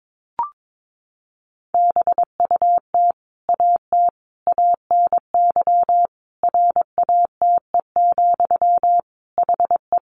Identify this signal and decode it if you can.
Morse code, heard as a keyed tone, about 22 wpm, 700 Hz. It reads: BUT AT ANY RATE, HE
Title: Sentences from Top 700 Words 22wpm